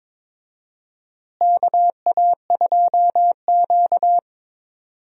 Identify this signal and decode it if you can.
Morse code, heard as a keyed tone, about 21 wpm, 700 Hz. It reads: KA2Q
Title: USA Call Signs - 4 Characters - Repeated 2x - 22wpm